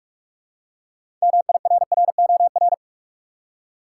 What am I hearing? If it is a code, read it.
Morse code, heard as a keyed tone, about 46 wpm, 700 Hz. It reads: MIRROR
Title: Six Letter Words - 45wpm